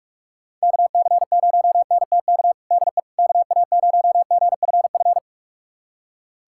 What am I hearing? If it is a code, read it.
Morse code, heard as a keyed tone, about 46 wpm, 700 Hz. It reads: KC0NTK DE KA0GFF
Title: USA Callsign DE Callsign 45wpm